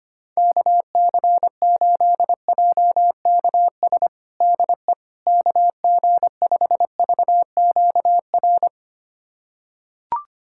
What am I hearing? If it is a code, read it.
Morse code, heard as a keyed tone, about 25 wpm, 700 Hz. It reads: KC8JKS DE KG5VQR
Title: USA Callsign DE Callsign - 6 Characters - 25wpm